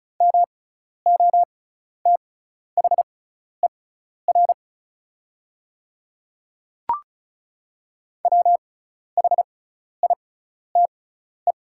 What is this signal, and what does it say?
Morse code, heard as a keyed tone, about 34 wpm, spaced as for 12 wpm, 700 Hz. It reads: MOTHER WHITE
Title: 6x Character Spacing - Top 500 Words - 35wpm